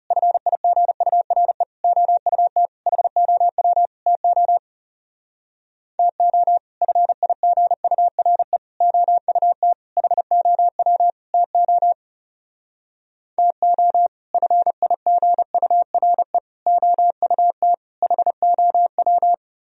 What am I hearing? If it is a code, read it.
Morse code, heard as a keyed tone, about 39 wpm, 700 Hz. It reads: FIGURE OUT HOW TO TO FIGURE OUT HOW TO TO FIGURE OUT HOW